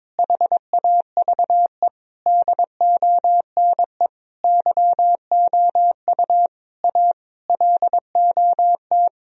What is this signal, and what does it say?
Morse code, heard as a keyed tone, about 22 wpm, 700 Hz. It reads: HAVE DONE YOU A LOT
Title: Sentences from Top 1000 Words - Review Part 3 - 22wpm